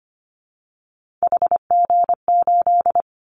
Morse code, heard as a keyed tone, about 25 wpm, 700 Hz. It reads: HG8
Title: Callsign Prefixes for Top 25 DXCC - Repeated 3x - 25wpm